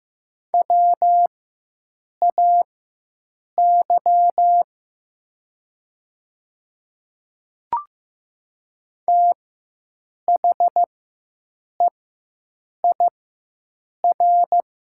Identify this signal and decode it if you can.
Morse code, heard as a keyed tone, about 15 wpm, spaced as for 7 wpm, 700 Hz. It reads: WAY THEIR
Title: 4x Character Spacing - Top 200 Words - 15wpm